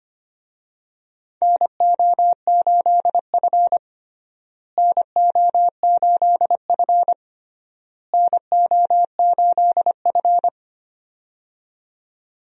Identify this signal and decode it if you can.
Morse code, heard as a keyed tone, about 25 wpm, 700 Hz. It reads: NO8F NO8F NO8F
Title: USA Call Signs - 4 Characters - Repeated 3x - 25wpm